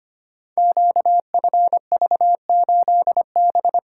Morse code, heard as a keyed tone, about 25 wpm, 700 Hz. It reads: QFV8B
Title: Code Groups - 5 Letter:Numbers - 25wpm